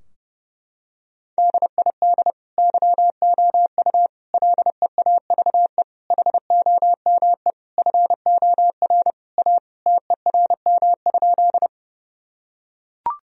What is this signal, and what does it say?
Morse code, heard as a keyed tone, about 30 wpm, 700 Hz. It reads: DID YOU LEAVE HOME FOR A TERM?
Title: Sentences from Top 1000 Words - Review Part 4 - 30wpm